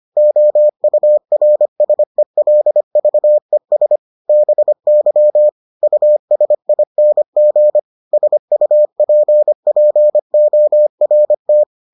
Morse code, heard as a keyed tone, about 25 wpm, 600 Hz. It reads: OURSELVES BY USING SUPPORT